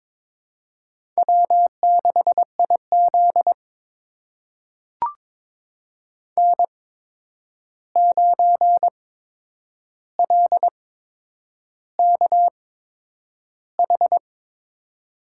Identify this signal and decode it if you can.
Morse code, heard as a keyed tone, about 22 wpm, spaced as for 9 wpm, 700 Hz. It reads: W6IZ N9LKH
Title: USA Call Signs - 8x Character Spacing - 22wpm